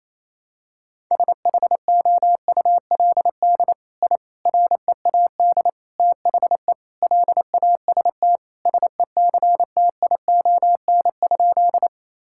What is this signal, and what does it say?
Morse code, heard as a keyed tone, about 28 wpm, 700 Hz. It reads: SHOULD I READ THE LAST SECTION?
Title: Sentences from Top 1000 Words - Review Part 5 - 28wpm